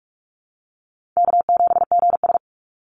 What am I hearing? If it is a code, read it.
Morse code, heard as a keyed tone, about 46 wpm, 700 Hz. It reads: K7GS
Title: Easy Call Signs - 45wpm